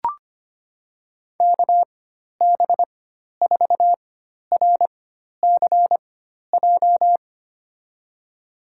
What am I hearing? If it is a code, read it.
Morse code, heard as a keyed tone, about 25 wpm, spaced as for 12 wpm, 700 Hz. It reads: KB4RCJ
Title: USA Call Signs - 6 Characters - 4x Character Spacing - 25wpm